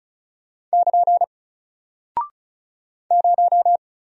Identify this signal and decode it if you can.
Morse code, heard as a keyed tone, about 36 wpm, 700 Hz. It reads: <KN> 0
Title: Single Character (Letter:Number:Punctuation:Prosign) Alt 35wpm